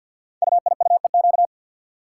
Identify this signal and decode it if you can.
Morse code, heard as a keyed tone, about 53 wpm, 700 Hz. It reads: UIUEQ
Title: Code Groups - 5 Letters - 50wpm